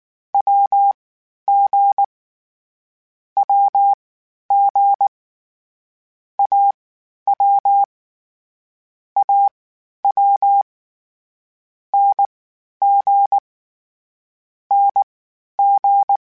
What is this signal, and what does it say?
Morse code, heard as a keyed tone, about 19 wpm, spaced as for 11 wpm, 800 Hz. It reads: WG WG AW AW NG NG